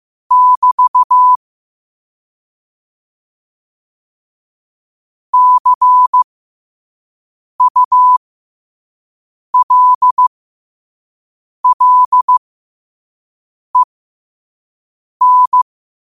Morse code, heard as a keyed tone, about 15 wpm, spaced as for 5 wpm, 1000 Hz. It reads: <BT> CULLEN
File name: qotdfiles / 5wpm